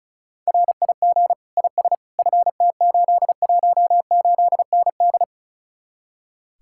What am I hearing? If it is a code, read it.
Morse code, heard as a keyed tone, about 44 wpm, 700 Hz. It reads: RIG IS FT818ND
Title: Speed-Racing Common QSO Phrases Encore - 35wpm